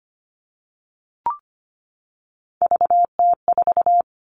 Morse code, heard as a keyed tone, about 25 wpm, 700 Hz. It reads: VT4
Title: All Cut Numbers - 3-Digit 25wpm